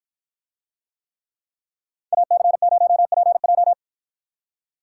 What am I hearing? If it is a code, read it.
Morse code, heard as a keyed tone, about 53 wpm, 700 Hz. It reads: AC0PJ